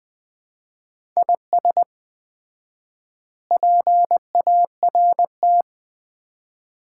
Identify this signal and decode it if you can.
Morse code, heard as a keyed tone, about 20 wpm, 700 Hz. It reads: IS PART
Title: 4x Word-Spacing - Sets of 4 Words - 20wpm